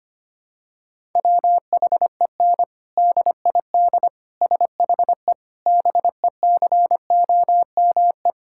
Morse code, heard as a keyed tone, about 25 wpm, 700 Hz. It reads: WHEN DID SHE BECOME
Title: Sentences from Top 1000 Words - Review Part 6 - 25wpm